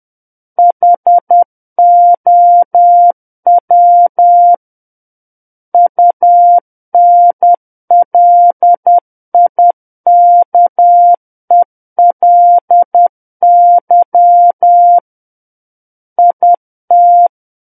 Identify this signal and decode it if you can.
Morse code, heard as a keyed tone, about 10 wpm, 700 Hz. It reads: HOW UNLIKELY IT
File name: Book One / 01-Chapter 1 - 10 WPM